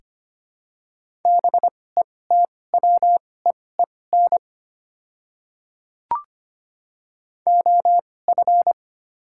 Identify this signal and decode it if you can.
Morse code, heard as a keyed tone, about 25 wpm, spaced as for 18 wpm, 700 Hz. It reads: BETWEEN OF
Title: 2x Character Spacing - Top 500 Words - 25wpm